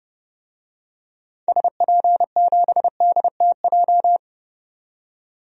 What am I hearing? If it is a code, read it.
Morse code, heard as a keyed tone, about 30 wpm, 700 Hz. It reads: SP7DTJ
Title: DXCC-100 Call Signs - 6 Characters - 30wpm